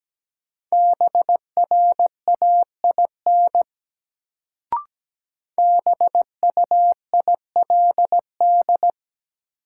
Morse code, heard as a keyed tone, about 17 wpm, 700 Hz. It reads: BRAIN BUILD
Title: Sentences from Top 1100 Words - New Word Drill Alt - 17wpm